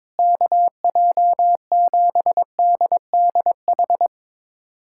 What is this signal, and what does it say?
Morse code, heard as a keyed tone, about 22 wpm, 700 Hz. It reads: KJ7DDH
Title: USA Callsign DE Callsign - 6 Characters - 22wpm